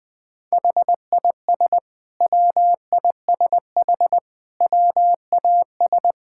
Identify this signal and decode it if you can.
Morse code, heard as a keyed tone, about 20 wpm, 700 Hz. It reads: HIS WISH WAS
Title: Sentences from Top 1600 Words 20wpm